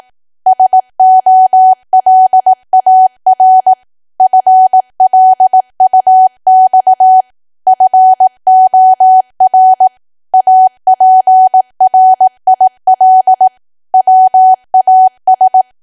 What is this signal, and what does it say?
Morse code, heard as a keyed tone, about 18 wpm, 750 Hz. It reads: SOLAR FLUX FOR APRIL WAS
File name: cwa / W1AW-Level2 / 18WPM